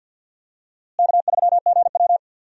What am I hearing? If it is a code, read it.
Morse code, heard as a keyed tone, about 48 wpm, 700 Hz. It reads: K3GW
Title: USA Call Signs - 4 Characters - Repeated 3x - 50wpm